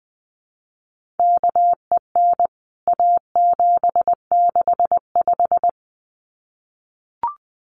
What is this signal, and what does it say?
Morse code, heard as a keyed tone, about 20 wpm, 700 Hz. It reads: KEN A765
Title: CWT Exchange 20wpm